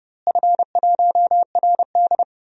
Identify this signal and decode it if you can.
Morse code, heard as a keyed tone, about 32 wpm, 700 Hz. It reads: F1RD
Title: DXCC-100 Call Signs - 4 Characters - Repeated 2x - 30wpm